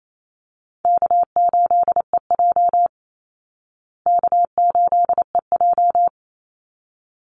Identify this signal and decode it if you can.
Morse code, heard as a keyed tone, about 28 wpm, 700 Hz. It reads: K8EJ K8EJ
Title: Easy Call Signs Repeated 2x - 28wpm